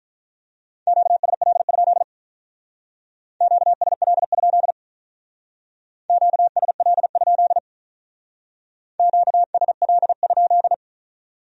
Morse code, heard as a keyed tone, about 44 wpm, 700 Hz. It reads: QSL? QSL? QSL? QSL?